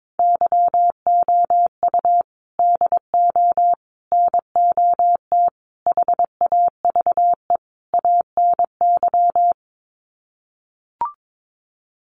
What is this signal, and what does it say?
Morse code, heard as a keyed tone, about 22 wpm, 700 Hz. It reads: YOU DO NOT HAVE ANY
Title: Sets of 5 Words Encore 22wpm